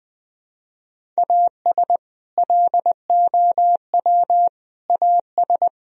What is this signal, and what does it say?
Morse code, heard as a keyed tone, about 21 wpm, 700 Hz. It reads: AS LOW AS
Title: Sentences from Top 1000 Words - Review Part 1 - 20wpm